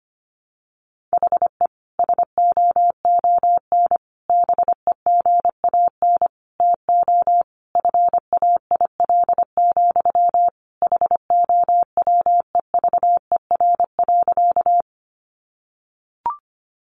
Morse code, heard as a keyed tone, about 25 wpm, 700 Hz. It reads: HE SOON BEGAN TO FAIL, HOWEVER.
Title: Sentences from Top 1100 Words 25wpm